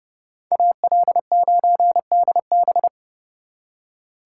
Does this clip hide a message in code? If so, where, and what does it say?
Yes: Morse code from 0.5 to 2.9 s, heard as a keyed tone, about 39 wpm, 700 Hz. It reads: AL9DB